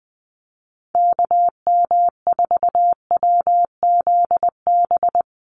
Morse code, heard as a keyed tone, about 20 wpm, 700 Hz. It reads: KM4WZB
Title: USA Call Signs Repeated 3x 20wpm